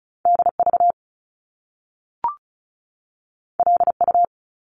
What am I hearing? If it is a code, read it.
Morse code, heard as a keyed tone, about 36 wpm, 700 Hz. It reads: DV LU
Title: Top Two Letter Combinations in Words 35wpm